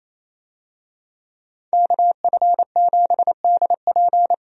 Morse code, heard as a keyed tone, about 28 wpm, 700 Hz. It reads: KF7DP
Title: USA Call Signs - 5 Characters - Repeated 2x - 28wpm